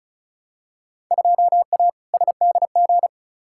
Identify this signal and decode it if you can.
Morse code, heard as a keyed tone, about 34 wpm, 700 Hz. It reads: 2A SDG
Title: Field Day Exchanges Encore - 35wpm